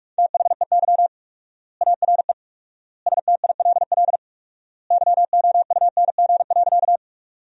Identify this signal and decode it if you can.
Morse code, heard as a keyed tone, about 46 wpm, 700 Hz. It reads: THEY ARE STILL YOUNG.
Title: 4x Word-Spacing - Sentences from Top 400 Words - 45wpm